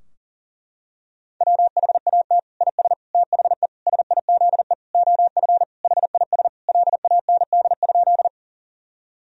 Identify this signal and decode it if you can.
Morse code, heard as a keyed tone, about 40 wpm, 700 Hz. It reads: WHAT IS THE SIZE OF HIS LAND?
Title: Sentences from Top 1000 Words - Review Part 2 - 40wpm